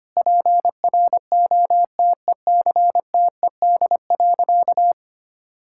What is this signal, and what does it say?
Morse code, heard as a keyed tone, about 25 wpm, 700 Hz. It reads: PROTECTED.
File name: practice / 25wpm / Sentences with 9-Word Phrases - 25wpm